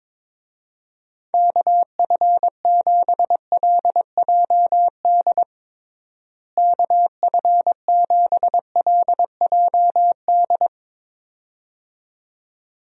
Morse code, heard as a keyed tone, about 22 wpm, 700 Hz. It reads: KF7LJD KF7LJD
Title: USA Call Signs - 6 Characters - Repeated 2x - 22wpm